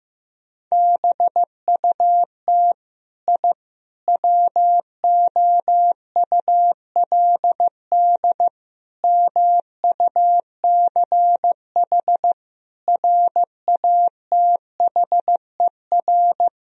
Morse code, heard as a keyed tone, about 15 wpm, 700 Hz. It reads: BUT I WOULD MUCH RATHER
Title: Sentences from Top 1000 Words - Review Part 4 - 15wpm